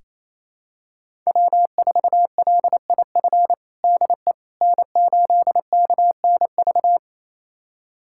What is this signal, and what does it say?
Morse code, heard as a keyed tone, about 28 wpm, 700 Hz. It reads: W4LIF DE N8KNV
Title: USA Callsign DE Callsign - 5 Characters - 28wpm